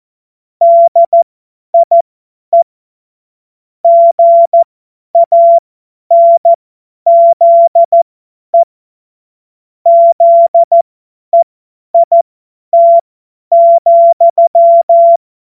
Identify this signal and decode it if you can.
Morse code, heard as a keyed tone, about 14 wpm, spaced as for 10 wpm, 680 Hz. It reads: DIE GANZE ZEIT,
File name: Morsekurs / Witze / Witz 25. Oktober 2022 50 F70